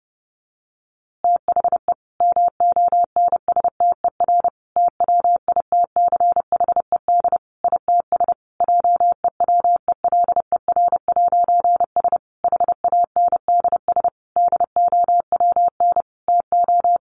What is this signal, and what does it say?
Morse code, heard as a keyed tone, about 30 wpm, 700 Hz. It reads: THE MONSTER TWITCHED ITS JEWELER'S HANDS DOWN TO
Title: A Sound Of Thunder 30wpm